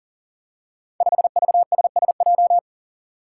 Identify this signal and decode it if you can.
Morse code, heard as a keyed tone, about 41 wpm, 700 Hz. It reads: 5VSSJ
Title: Five Challenging Letter-Number Combinations 40wpm